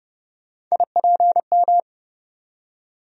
Morse code, heard as a keyed tone, about 31 wpm, 700 Hz. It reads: IPM
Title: USA Call Sign Postfixes - 3 Letters 30wpm